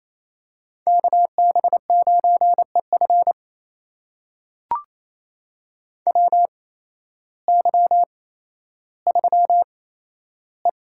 Morse code, heard as a keyed tone, about 28 wpm, spaced as for 13 wpm, 700 Hz. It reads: KB9EF WY3E
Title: USA Call Signs - 8x Character Spacing - 28wpm